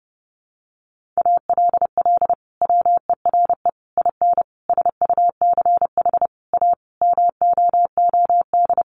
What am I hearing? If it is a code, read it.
Morse code, heard as a keyed tone, about 30 wpm, 700 Hz. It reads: ALL WERE IN SUCH A MOOD
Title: Sentences from Top 1100 Words 30wpm